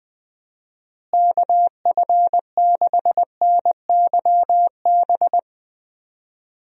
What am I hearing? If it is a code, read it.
Morse code, heard as a keyed tone, about 21 wpm, 700 Hz. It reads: KF6NYB
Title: USA Call Signs - 6 Characters - Repeated 3x - 20wpm